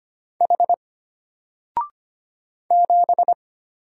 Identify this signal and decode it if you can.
Morse code, heard as a keyed tone, about 25 wpm, 700 Hz. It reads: H 7
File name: Single-Letter-Number-25wpm